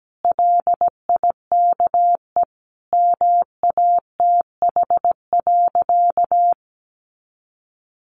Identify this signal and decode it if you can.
Morse code, heard as a keyed tone, about 17 wpm, 700 Hz. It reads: LIKE MATH.
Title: Easy Phrases 17wpm